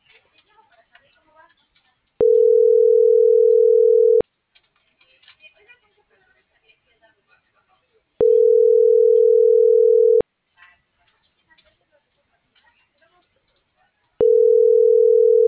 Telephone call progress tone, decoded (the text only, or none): ringback tone